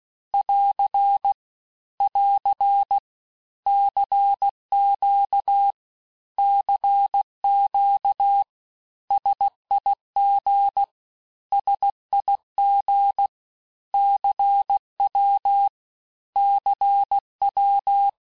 Morse code, heard as a keyed tone, about 16 wpm, 800 Hz. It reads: <AR> <AR> CQ CQ SIG SIG CW CW
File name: cours-du-lundi files / cours-1592--teste-abreviations-vitesse-15mpm-puis-20mpm